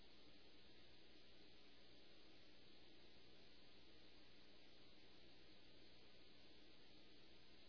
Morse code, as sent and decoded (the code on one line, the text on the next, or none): none